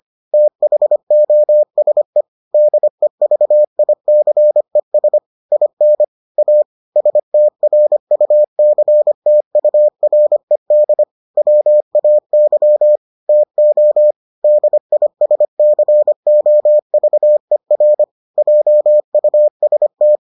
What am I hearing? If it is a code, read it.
Morse code, heard as a keyed tone, about 25 wpm, 600 Hz. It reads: THOSE DEVICES IN A STRUCTURED WAY TO DISCOVER JUST